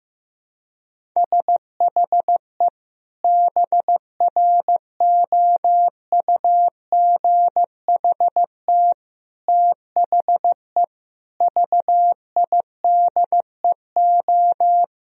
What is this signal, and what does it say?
Morse code, heard as a keyed tone, about 15 wpm, 700 Hz. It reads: SHE BROUGHT THE VIDEO
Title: Sentences from Top 1000 Words - Review Part 4 - 15wpm